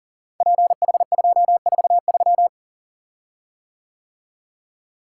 Morse code, heard as a keyed tone, about 42 wpm, 700 Hz. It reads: PH243
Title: Code Groups - 5 Letter:Numbers - 40wpm